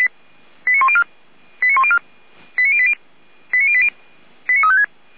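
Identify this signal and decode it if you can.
Five-tone selective call (ZVEI), heard as quick runs of five tones, tones of about 70 ms. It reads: 89104 89104 89080 89080 89356
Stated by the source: ZVEI five-tone selective call received by SDR.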